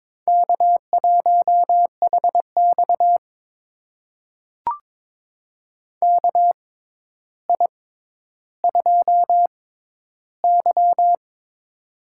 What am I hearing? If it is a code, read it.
Morse code, heard as a keyed tone, about 22 wpm, spaced as for 11 wpm, 700 Hz. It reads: K1HX KI2Y
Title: USA Call Signs - 4 Characters - 6x Character Spacing - 22wpm